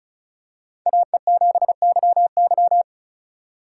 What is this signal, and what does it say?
Morse code, heard as a keyed tone, about 36 wpm, 700 Hz. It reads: AE7YY
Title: USA Call Signs - 5 Characters - 6x Character Spacing - 35wpm